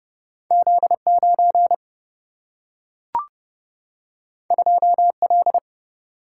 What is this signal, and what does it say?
Morse code, heard as a keyed tone, about 30 wpm, 700 Hz. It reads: Z 9 2 L
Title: Single-Letter Single-Number 30wpm